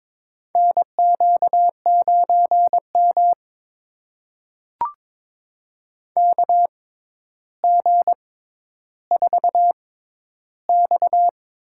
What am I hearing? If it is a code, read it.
Morse code, heard as a keyed tone, about 22 wpm, spaced as for 11 wpm, 700 Hz. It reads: NQ9M KG4X
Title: USA Call Signs - 6x Character Spacing - 22wpm